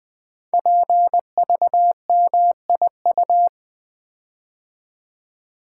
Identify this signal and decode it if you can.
Morse code, heard as a keyed tone, about 20 wpm, 700 Hz. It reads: PVMIU
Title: Code Groups - 5 Letter:Numbers - 20wpm